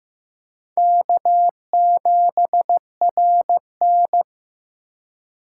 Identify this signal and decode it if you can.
Morse code, heard as a keyed tone, about 18 wpm, 700 Hz. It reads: K7RN
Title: Speed Racing - USA 1x2 Call Signs 15wpm